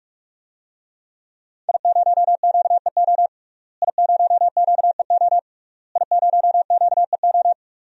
Morse code, heard as a keyed tone, about 46 wpm, 700 Hz. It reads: I0QEO I0QEO I0QEO
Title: DXCC-100 Call Signs - 5 Characters - Repeated 3x - 45wpm